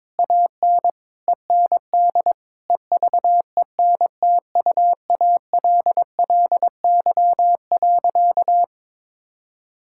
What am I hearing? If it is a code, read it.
Morse code, heard as a keyed tone, about 22 wpm, 700 Hz. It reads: AN END EVENTUALLY.
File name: practice / 22wpm / Sentences with 8-Word Phrases - 22wpm